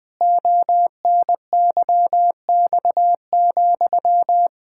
Morse code, heard as a keyed tone, about 20 wpm, 700 Hz. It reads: ONYX,
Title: Pangram Challenge v2 - 20wpm